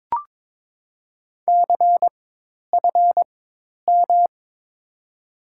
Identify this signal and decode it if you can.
Morse code, heard as a keyed tone, about 22 wpm, spaced as for 10 wpm, 700 Hz. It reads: CFM
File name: Top 90 QSO Elements - 4x Character Spacing - 22wpm